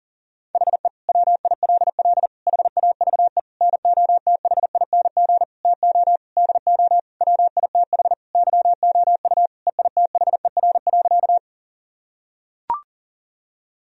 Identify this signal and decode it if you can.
Morse code, heard as a keyed tone, about 40 wpm, 700 Hz. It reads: HE WILL HAVE NOTHING TO DO WITH YOU EITHER.
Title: Sentences from Top 1000 Words - Review Part 4 - 40wpm